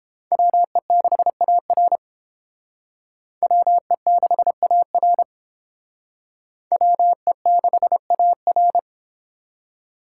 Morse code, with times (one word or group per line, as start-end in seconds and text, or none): 0.3-2.0 WE6AR
3.4-5.2 WE6AR
6.7-8.8 WE6AR